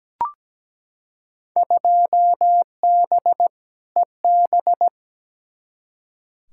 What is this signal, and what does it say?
Morse code, heard as a keyed tone, about 17 wpm, 700 Hz. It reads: 2B EB
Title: Field Day Exchanges - 17wpm